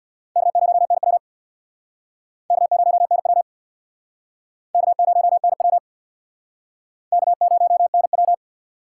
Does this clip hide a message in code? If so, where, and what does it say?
Yes: Morse code from 0.4 to 8.3 s, heard as a keyed tone, about 64 wpm, 700 Hz. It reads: K0NW K0NW K0NW K0NW